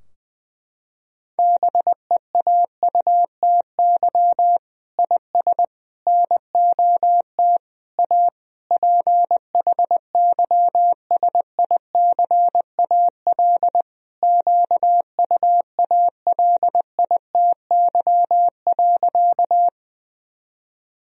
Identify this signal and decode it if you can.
Morse code, heard as a keyed tone, about 20 wpm, 700 Hz. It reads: BEAUTY IS NOT A PHYSICAL QUALITY.
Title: Sentences from Top 1000 Words - Review Part 4 - 20wpm